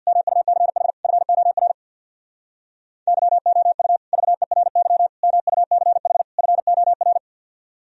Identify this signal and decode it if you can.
Morse code, heard as a keyed tone, about 52 wpm, 700 Hz. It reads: MUCH FOR YOU VERY MUCH FOR
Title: Speed-Racing - Sets of 4 Words 50wpm